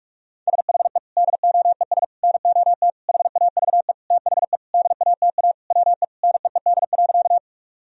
Morse code, heard as a keyed tone, about 46 wpm, 700 Hz. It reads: SHE DOES NOT HAVE THE DATA WE NEED.
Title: Sentences from Top 1000 Words - Review Part 3 - 45wpm